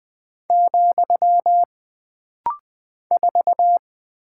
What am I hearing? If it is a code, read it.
Morse code, heard as a keyed tone, about 20 wpm, 700 Hz. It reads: , 4
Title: Single Character (Letter:Number:Punctuation:Prosign) - Rapid-Fire - 20wpm